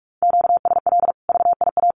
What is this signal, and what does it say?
Morse code, heard as a keyed tone, about 44 wpm, 700 Hz. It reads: QSL VIA
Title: Common QSO Phrases 45wpm